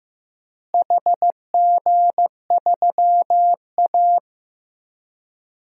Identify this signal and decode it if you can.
Morse code, heard as a keyed tone, about 15 wpm, 700 Hz. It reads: HG3A
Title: DXCC-100 Call Signs - 4 Characters - Repeated 2x - 15wpm